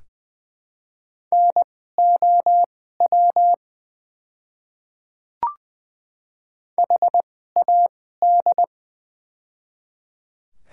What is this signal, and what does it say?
Morse code, heard as a keyed tone, about 20 wpm, spaced as for 14 wpm, 700 Hz. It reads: NOW HAD
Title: 2x Character Spacing - Top 500 Words - 20wpm